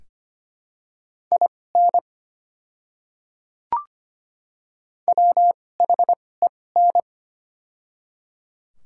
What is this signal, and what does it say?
Morse code, heard as a keyed tone, about 25 wpm, spaced as for 18 wpm, 700 Hz. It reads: IN WHEN
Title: 2x Character Spacing - Top 100 Words - 25wpm